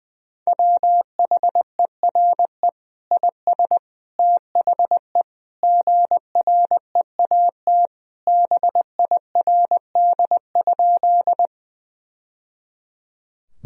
Morse code, heard as a keyed tone, about 20 wpm, 700 Hz. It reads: WHERE IS THE GREAT BIRD?
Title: Sentences from Top 1000 Words - Review Part 2 - 20wpm